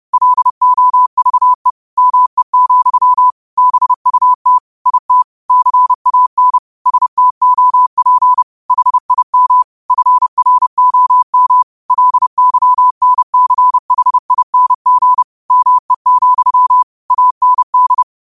Morse code, heard as a keyed tone, about 30 wpm, 1000 Hz. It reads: LOVE ME, BUT IT CAN STOP HIM FROM LYNCHING ME, AND
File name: qotdfiles / 30wpm